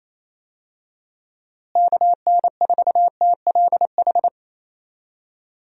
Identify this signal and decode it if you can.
Morse code, heard as a keyed tone, about 27 wpm, 700 Hz. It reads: KN4TLH
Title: USA Call Signs - 6 Characters - Repeated 3x - 28wpm